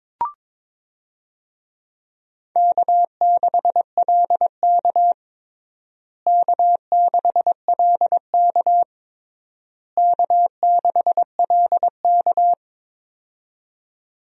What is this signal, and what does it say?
Morse code, heard as a keyed tone, about 22 wpm, 700 Hz. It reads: K6LK K6LK K6LK
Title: USA Call Signs - 4 Characters - Repeated 3x - 22wpm